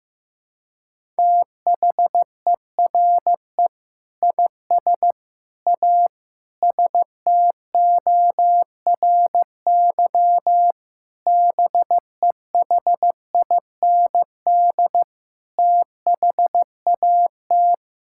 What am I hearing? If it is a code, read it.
Morse code, heard as a keyed tone, about 15 wpm, 700 Hz. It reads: THERE IS A STORY BEHIND THAT
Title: Sentences from Top 1000 Words - Review Part 5 - 15wpm